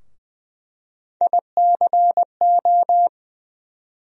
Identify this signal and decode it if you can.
Morse code, heard as a keyed tone, about 20 wpm, 700 Hz. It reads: ICO
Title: USA Call Sign Postfixes - 3 Letters 20wpm